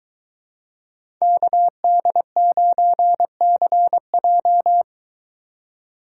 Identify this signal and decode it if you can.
Morse code, heard as a keyed tone, about 22 wpm, 700 Hz. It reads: KD9CJ